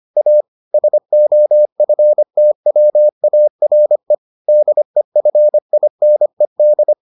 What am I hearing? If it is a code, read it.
Morse code, heard as a keyed tone, about 25 wpm, 600 Hz. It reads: A SOFTWARE DEFINED